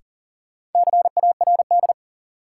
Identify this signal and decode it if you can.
Morse code, heard as a keyed tone, about 40 wpm, 700 Hz. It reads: CARD
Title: Sentences from Top 600 Words - New Word Drill - 40wpm